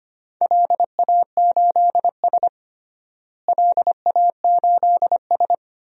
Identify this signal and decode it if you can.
Morse code, heard as a keyed tone, about 25 wpm, 700 Hz. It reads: LA8S LA8S
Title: DXCC-100 Call Signs - 4 Characters - Repeated 2x - 25wpm